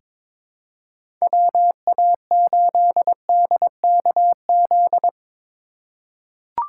Morse code, heard as a keyed tone, about 22 wpm, 700 Hz. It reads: WA8DKZ
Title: USA Call Signs - 6 Characters - 3x Character Spacing - 22wpm